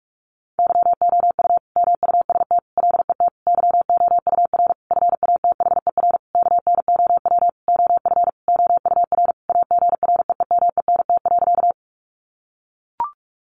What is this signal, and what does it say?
Morse code, heard as a keyed tone, about 44 wpm, 700 Hz. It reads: YOU MUST LET YOUR FATHER KNOW OF OUR AGREEMENT.
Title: Sentences from Top 1500 Words 45wpm